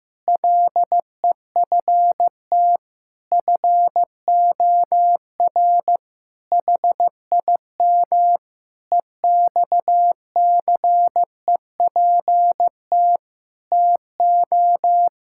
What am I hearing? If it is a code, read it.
Morse code, heard as a keyed tone, about 16 wpm, 700 Hz. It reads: LEFT FOR HIM EXCEPT TO
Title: Sentences from Top 1600 Words 15wpm